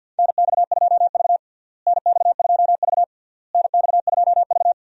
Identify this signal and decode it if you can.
Morse code, heard as a keyed tone, about 51 wpm, 700 Hz. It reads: NX2V NX2V NX2V
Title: USA Call Signs - 4 Characters - Repeated 3x - 50wpm